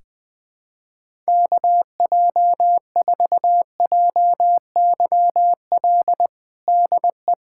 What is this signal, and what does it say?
Morse code, heard as a keyed tone, about 20 wpm, 700 Hz. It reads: KJ4JYL DE
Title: USA Callsign DE Callsign 20wpm